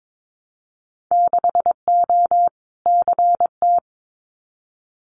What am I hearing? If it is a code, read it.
Morse code, heard as a keyed tone, about 22 wpm, 700 Hz. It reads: 6O CT
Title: Winter Field Day Exchanges Encore 22wpm